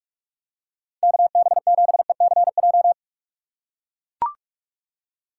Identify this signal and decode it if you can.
Morse code, heard as a keyed tone, about 41 wpm, 700 Hz. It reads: KB7ECJ
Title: USA Call Signs - 6 Characters - 2x Character Spacing - 45wpm